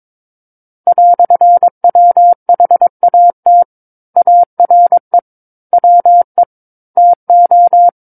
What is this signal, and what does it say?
Morse code, heard as a keyed tone, about 22 wpm, 700 Hz. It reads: "WHAT ARE WE TO
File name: Book One / 13-Chapter 13 - 22 WPM